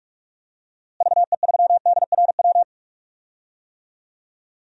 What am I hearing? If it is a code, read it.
Morse code, heard as a keyed tone, about 46 wpm, 700 Hz. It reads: VE3DRW
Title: DXCC-100 Call Signs - 6 Characters - Repeated 3x - 45wpm